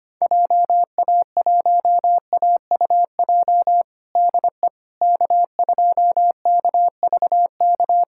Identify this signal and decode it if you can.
Morse code, heard as a keyed tone, about 25 wpm, 700 Hz. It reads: JA1AUJ DE K2KVK
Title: USA Callsign DE Callsign 25wpm